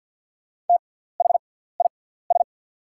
Morse code, heard as a keyed tone, about 48 wpm, spaced as for 17 wpm, 700 Hz. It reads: THIS
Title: 6x Character Spacing - Top 100 Words - 50wpm